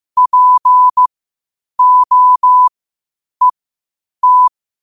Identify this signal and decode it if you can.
Morse code, heard as a keyed tone, about 15 wpm, spaced as for 8 wpm, 1000 Hz. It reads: POET
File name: qotdfiles / 8wpm